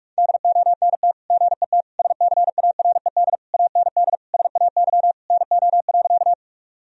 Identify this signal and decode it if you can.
Morse code, heard as a keyed tone, about 44 wpm, 700 Hz. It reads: DONT GET SCARED AND SAY NO.
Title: Sentences from Top 1000 Words - Review Part 4 - 45wpm